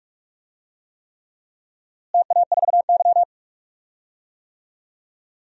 Morse code, heard as a keyed tone, about 43 wpm, 700 Hz. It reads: TA4Y
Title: DXCC-100 Call Signs - 4 Characters - Repeated 3x - 45wpm